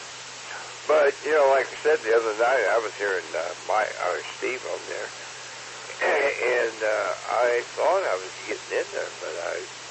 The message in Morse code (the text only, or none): none